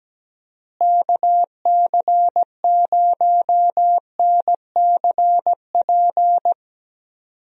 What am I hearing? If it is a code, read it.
Morse code, heard as a keyed tone, about 17 wpm, 700 Hz. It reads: KC0NCP
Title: USA Call Signs - 6 Characters - Repeated 3x - 17wpm